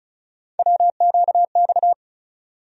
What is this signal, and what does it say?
Morse code, heard as a keyed tone, about 34 wpm, 700 Hz. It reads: WQX
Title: USA Call Sign Postfixes - 3 Letters 35wpm